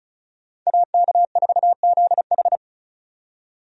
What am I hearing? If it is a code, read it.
Morse code, heard as a keyed tone, about 36 wpm, 700 Hz. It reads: AK4ZH
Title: USA Call Signs - 5 Characters - 35wpm